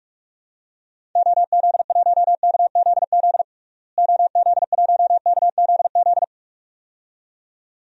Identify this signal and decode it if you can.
Morse code, heard as a keyed tone, about 46 wpm, 700 Hz. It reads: OZ1KZZ OZ1KZZ
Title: DXCC-100 Call Signs - 6 Characters - Repeated 2x - 45wpm